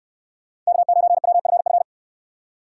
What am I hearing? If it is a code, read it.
Morse code, heard as a keyed tone, about 62 wpm, 700 Hz. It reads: K9WFF